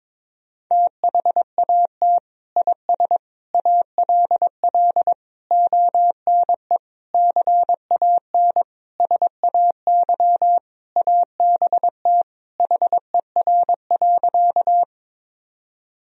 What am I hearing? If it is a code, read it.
Morse code, heard as a keyed tone, about 21 wpm, 700 Hz. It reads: THAT IS ALL ONE CAN SAY ABT HER.
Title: Sentences from Top 200 Words and Abbreviations 22wpm